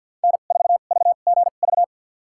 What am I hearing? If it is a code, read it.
Morse code, heard as a keyed tone, about 47 wpm, spaced as for 35 wpm, 700 Hz. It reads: N4VGV
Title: USA Call Signs - 5 Characters - 2x Character Spacing - 50wpm